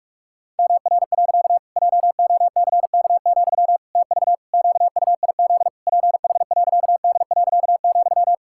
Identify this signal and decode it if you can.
Morse code, heard as a keyed tone, about 41 wpm, 700 Hz. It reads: MR. JOCK, TV QUIZ PH.D.,
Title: Pangram Challenge v2 - 45wpm